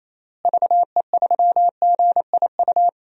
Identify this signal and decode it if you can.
Morse code, heard as a keyed tone, about 28 wpm, 700 Hz. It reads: VE3GIU
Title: DXCC-100 Call Signs - 6 Characters - Repeated 3x - 28wpm